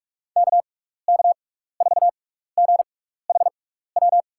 Morse code, heard as a keyed tone, about 47 wpm, spaced as for 16 wpm, 700 Hz. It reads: KK4GHW
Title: USA Call Signs - 6x Character Spacing - 45wpm